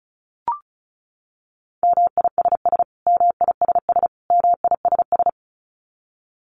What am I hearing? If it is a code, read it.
Morse code, heard as a keyed tone, about 34 wpm, 700 Hz. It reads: MISS MISS MISS
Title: Top 500 Words Repeated 35wpm